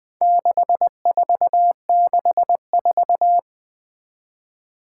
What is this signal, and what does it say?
Morse code, heard as a keyed tone, about 21 wpm, 700 Hz. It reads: 6464
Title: Four Challenging Letter-Number Combinations 20wpm